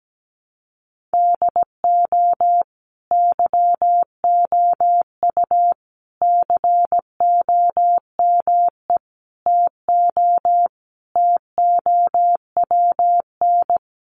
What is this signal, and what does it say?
Morse code, heard as a keyed tone, about 17 wpm, 700 Hz. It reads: DO YOU COME TO TOWN
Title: Sentences from Top 1400 Words 17wpm